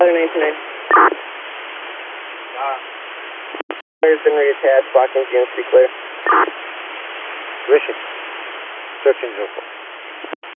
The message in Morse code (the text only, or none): none